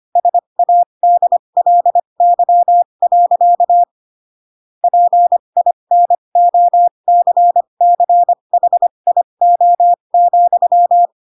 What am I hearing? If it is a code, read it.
Morse code, heard as a keyed tone, about 25 wpm, spaced as for 22 wpm, 700 Hz. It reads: SADLY. PINOCCHIO,